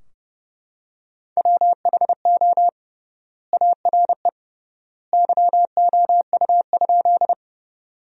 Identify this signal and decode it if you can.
Morse code, heard as a keyed tone, about 30 wpm, 700 Hz. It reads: WHO ARE YOU?
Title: 3x Word-Spacing - Sentences from Top 100 Words - 30wpm